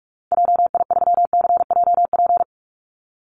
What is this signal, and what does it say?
Morse code, heard as a keyed tone, about 44 wpm, 700 Hz. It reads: JI3CJP
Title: Call Signs Repeated 3x - 45wpm